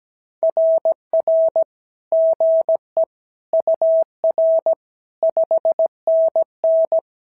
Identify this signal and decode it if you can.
Morse code, heard as a keyed tone, about 17 wpm, 650 Hz. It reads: RR GE UR 5NN
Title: SOTA QSO Practice 17wpm